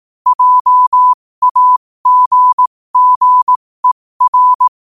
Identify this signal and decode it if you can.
Morse code, heard as a keyed tone, about 18 wpm, spaced as for 15 wpm, 1000 Hz. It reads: JAGGER
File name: qotdfiles / 15wpm